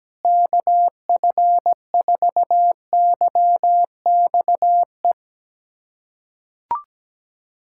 Morse code, heard as a keyed tone, about 17 wpm, 700 Hz. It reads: KF4YXE
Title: USA Callsign DE Callsign - 6 Characters - 17wpm